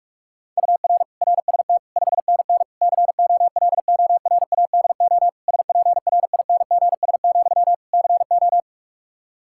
Morse code, heard as a keyed tone, about 46 wpm, 700 Hz. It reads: UR RST 5NN COLORADO SPRINGS, CO
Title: Practice QSO - Exchange 1 - Beginning - 45wpm